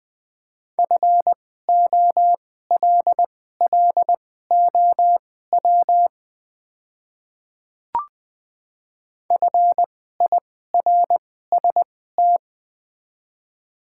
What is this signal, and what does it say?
Morse code, heard as a keyed tone, about 20 wpm, spaced as for 14 wpm, 700 Hz. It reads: FOLLOW FIRST
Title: 2x Character Spacing - Top 200 Words - 20wpm